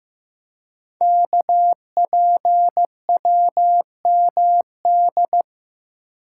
Morse code, heard as a keyed tone, about 15 wpm, 700 Hz. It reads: KPWMD
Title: Code Groups - 5 Letters - 15wpm